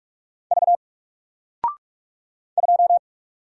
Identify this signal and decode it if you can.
Morse code, heard as a keyed tone, about 48 wpm, 700 Hz. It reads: V 2
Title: Single Character (Letter:Number:Punctuation:Prosign) Alt 45wpm